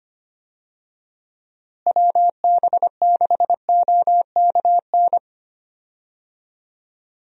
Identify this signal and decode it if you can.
Morse code, heard as a keyed tone, about 25 wpm, 700 Hz. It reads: WB6OKN
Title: USA Call Signs - 6 Characters - Repeated 3x - 25wpm